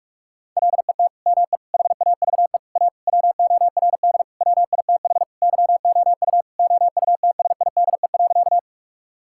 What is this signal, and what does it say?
Morse code, heard as a keyed tone, about 46 wpm, 700 Hz. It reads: LET ME HAVE A WORD WITH YOU OUTSIDE.
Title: Sentences from Top 1000 Words - Review Part 6 - 45wpm